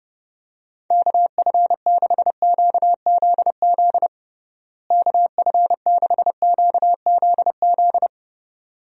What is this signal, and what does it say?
Morse code, heard as a keyed tone, about 30 wpm, 700 Hz. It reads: KF6QZZ KF6QZZ
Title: USA Call Signs - 6 Characters - Repeated 2x - 30wpm